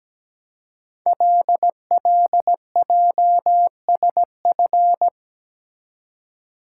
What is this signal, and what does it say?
Morse code, heard as a keyed tone, about 17 wpm, 700 Hz. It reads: LLJSF
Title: Code Groups - 5 Letters - 17wpm